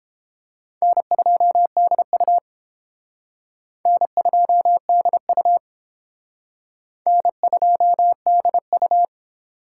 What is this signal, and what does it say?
Morse code, heard as a keyed tone, about 30 wpm, 700 Hz. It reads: N2DU N2DU N2DU